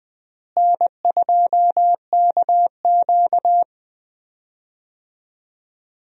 Morse code, heard as a keyed tone, about 20 wpm, 700 Hz. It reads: N2KQ
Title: USA Call Signs - 4 Characters - Repeated 2x - 20wpm